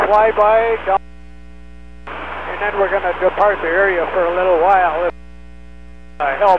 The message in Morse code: none

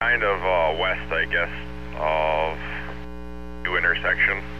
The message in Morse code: none

